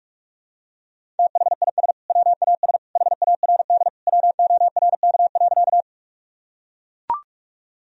Morse code, heard as a keyed tone, about 45 wpm, 700 Hz. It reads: THIS WAS HARD WORK.
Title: Sentences from Top 1000 Words - Review Part 1 - 45wpm